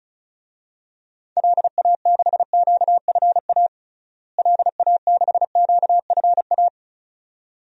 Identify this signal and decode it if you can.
Morse code, heard as a keyed tone, about 34 wpm, 700 Hz. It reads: LA6QFA LA6QFA
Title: DXCC-100 Call Signs - 6 Characters - Repeated 2x - 35wpm